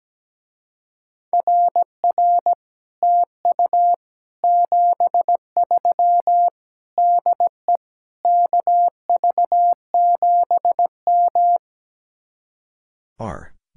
Morse code, heard as a keyed tone, about 17 wpm, 700 Hz. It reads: RR TU 73 DE KV7M
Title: Speed-Racing Common QSO Phrases - 17wpm